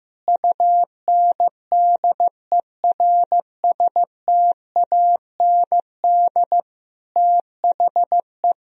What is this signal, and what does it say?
Morse code, heard as a keyed tone, about 15 wpm, 700 Hz. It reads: UNDERSTAND THE
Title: Sentences from Top 1000 Words - Review Part 3 - 15wpm